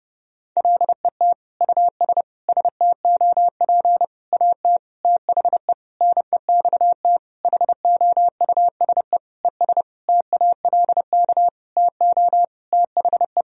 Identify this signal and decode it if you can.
Morse code, heard as a keyed tone, about 30 wpm, 700 Hz. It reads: LET US STOP AT THE NEXT HOUSE ES TALK TO THE
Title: Sentences from Top 400 Words and Abbreviations 30wpm